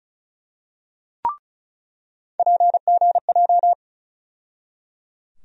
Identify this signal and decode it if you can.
Morse code, heard as a keyed tone, about 36 wpm, 700 Hz. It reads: PGJ
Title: USA Call Sign Postfixes - 3 Letters 35wpm